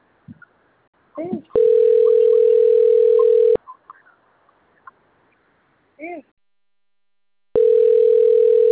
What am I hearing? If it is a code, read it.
Ringback tone (North America), 440 + 480 Hz, with speech.
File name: rg-1600-unknown- - - .63807